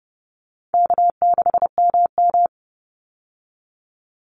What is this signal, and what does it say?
Morse code, heard as a keyed tone, about 30 wpm, 700 Hz. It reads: K6MM
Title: Easy Call Signs Repeated 2x - 30wpm